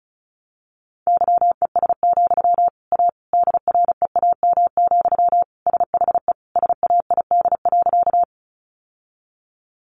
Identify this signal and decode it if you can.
Morse code, heard as a keyed tone, about 34 wpm, 700 Hz. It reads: YES, A DREAM, SHE SAID.
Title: Sentences from Top 1500 Words 35wpm